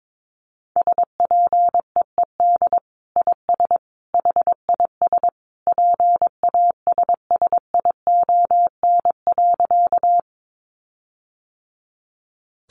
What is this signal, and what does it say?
Morse code, heard as a keyed tone, about 22 wpm, 700 Hz. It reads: SPEED IS HIS PASSION.
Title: Sentences from Top 1300 Words 22wpm